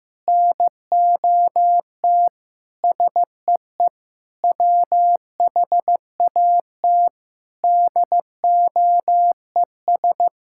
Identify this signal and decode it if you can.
Morse code, heard as a keyed tone, about 15 wpm, 700 Hz. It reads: NOT SEE WHAT DOES
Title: Sentences from Top 1000 Words - Review Part 5 - 15wpm